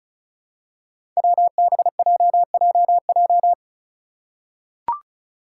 Morse code, heard as a keyed tone, about 36 wpm, 700 Hz. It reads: WBJJJ
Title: Five Challenging Letter-Number Combinations 35wpm